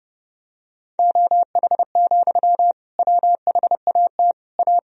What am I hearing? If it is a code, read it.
Morse code, heard as a keyed tone, about 30 wpm, 700 Hz. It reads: OH, WHAT A
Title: Sentences from Top 1000 Words - Review Part 2 - 30wpm